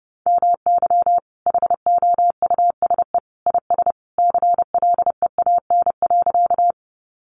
Morse code, heard as a keyed tone, about 30 wpm, 700 Hz. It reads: MY HOUSE IS CLEAN.
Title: Sentences from Top 1100 Words 30wpm